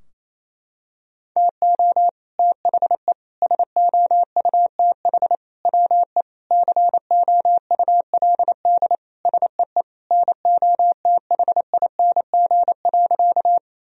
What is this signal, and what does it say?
Morse code, heard as a keyed tone, about 28 wpm, 700 Hz. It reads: TO THE SOUTH WE COULD SEE NOTHING.
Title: Sentences from Top 1000 Words - Review Part 2 - 28wpm